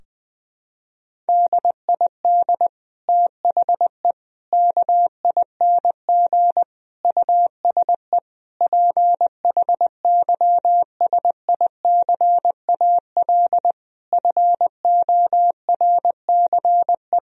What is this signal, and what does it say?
Morse code, heard as a keyed tone, about 20 wpm, 700 Hz. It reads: DID THE KING USE PHYSICAL FORCE
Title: Sentences from Top 1000 Words - Review Part 4 - 20wpm